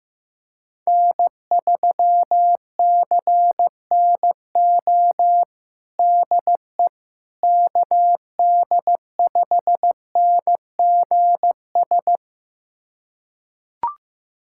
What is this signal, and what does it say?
Morse code, heard as a keyed tone, about 15 wpm, 700 Hz. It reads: N3CNO DE KD5NGS
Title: USA Callsign DE Callsign 15wpm